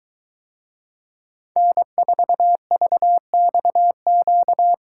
Morse code, heard as a keyed tone, about 22 wpm, 700 Hz. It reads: N4VXQ